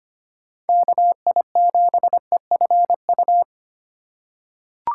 Morse code, heard as a keyed tone, about 25 wpm, 700 Hz. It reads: KI7EFU
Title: USA Call Signs - 4x Character Spacing - 25wpm